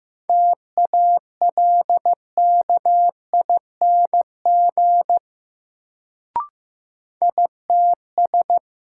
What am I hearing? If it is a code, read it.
Morse code, heard as a keyed tone, about 15 wpm, 700 Hz. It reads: TALKING ITS
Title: Sets of 2 Words - New Word Drill - 15wpm